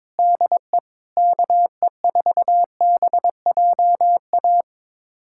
Morse code, heard as a keyed tone, about 21 wpm, 700 Hz. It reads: DE KE4BJA
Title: USA Callsign DE Callsign - 6 Characters - 22wpm